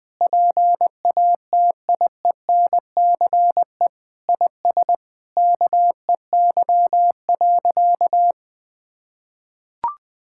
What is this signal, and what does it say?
Morse code, heard as a keyed tone, about 20 wpm, 700 Hz. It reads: PATIENCE IS KEY.
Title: Sentences with 7-Word Phrases - Spoken Beginning - 20wpm